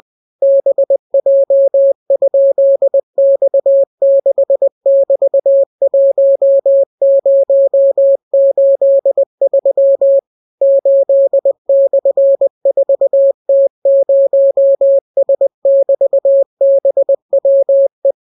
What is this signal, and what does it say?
Morse code, heard as a keyed tone, about 20 wpm, 550 Hz. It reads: BJ?X6<BT>1083 8/4T0S<BT>BWE